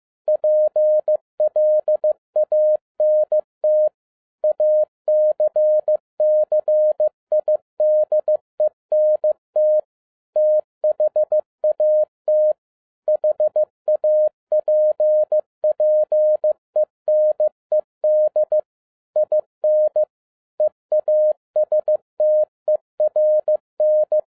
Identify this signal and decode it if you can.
Morse code, heard as a keyed tone, about 15 wpm, 610 Hz. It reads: PLANT ACCIDENT THAT HAPPENED IN EASTERN
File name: FIELD DAY CONTEST QRV FROM SAKAE VILLAGE 15wpm